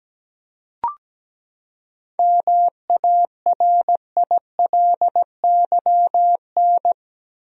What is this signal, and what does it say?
Morse code, heard as a keyed tone, about 17 wpm, 700 Hz. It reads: MARILYN
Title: USA Common Names - 17wpm